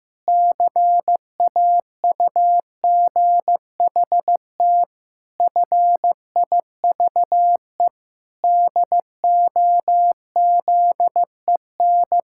Morse code, heard as a keyed tone, about 15 wpm, 700 Hz. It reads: CAUGHT FIVE DOZEN
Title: Pangram Challenge v2 - 15wpm